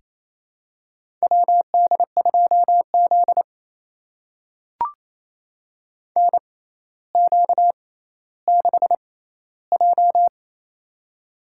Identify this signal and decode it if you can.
Morse code, heard as a keyed tone, about 28 wpm, spaced as for 14 wpm, 700 Hz. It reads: WD2Z NQ6J
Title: USA Call Signs - 4 Characters - 6x Character Spacing - 28wpm